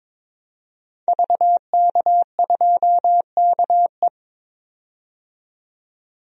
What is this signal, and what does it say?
Morse code, heard as a keyed tone, about 22 wpm, 700 Hz. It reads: VK2KE
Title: DXCC-100 Call Signs - 5 Characters - 22wpm